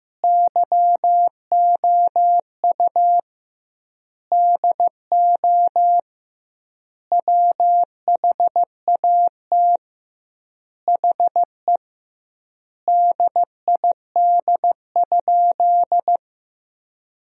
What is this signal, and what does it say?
Morse code, heard as a keyed tone, about 15 wpm, 700 Hz. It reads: YOU DO WHAT HE DID?
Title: 2x Word-Spacing - Sentences from Top 100 Words - 15wpm